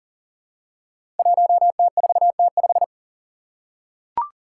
Morse code, heard as a keyed tone, about 41 wpm, 700 Hz. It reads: 1T4T5
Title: Five Challenging Letter-Number Combinations 40wpm